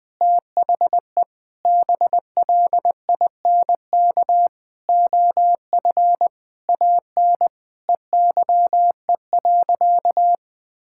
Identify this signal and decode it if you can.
Morse code, heard as a keyed tone, about 20 wpm, 700 Hz. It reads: THE BLINK OF AN EYE.
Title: Sentences with 6-Word Phrases - Spoken Beginning - 20wpm